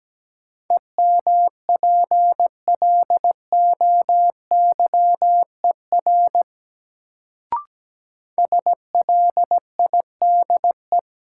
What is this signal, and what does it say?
Morse code, heard as a keyed tone, about 17 wpm, 700 Hz. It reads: EMPLOYER SLIDE
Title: Sentences from Top 1600 Words - New Word Drill Alt - 17wpm